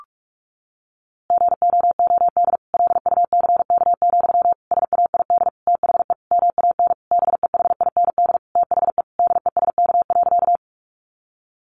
Morse code, heard as a keyed tone, about 44 wpm, 700 Hz. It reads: GOOD LUCK, SAID THE MAN BEHIND THE DESK.
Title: A Sound Of Thunder 45wpm